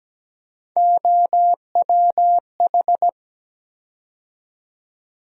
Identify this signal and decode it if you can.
Morse code, heard as a keyed tone, about 17 wpm, 700 Hz. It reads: OWH
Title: USA Call Sign Postfixes - 3 Letters 17wpm